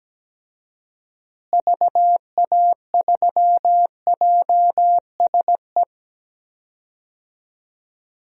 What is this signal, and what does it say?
Morse code, heard as a keyed tone, about 17 wpm, 700 Hz. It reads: VA3JSE
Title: DXCC-100 Call Signs - 6 Characters - Repeated 3x - 17wpm